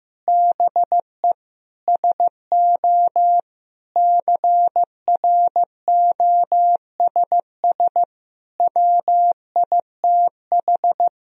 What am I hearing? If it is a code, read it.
Morse code, heard as a keyed tone, about 15 wpm, 700 Hz. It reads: BE SO CROSS WITH
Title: Sentences from Top 1000 Words - Review Part 1 - 15wpm